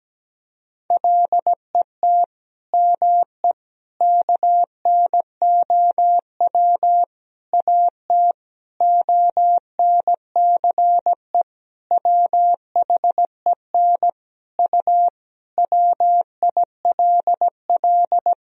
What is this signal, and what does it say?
Morse code, heard as a keyed tone, about 17 wpm, 700 Hz. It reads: LET ME KNOW AT ONCE WHEN U WILL
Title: Sentences from Top 300 Words and Abbreviations 17wpm